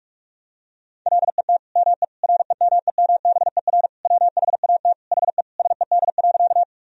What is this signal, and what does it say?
Morse code, heard as a keyed tone, about 41 wpm, 700 Hz. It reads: LET ME REMEMBER WHAT HE SED.
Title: Sentences from Top 400 Words and Abbreviations 45wpm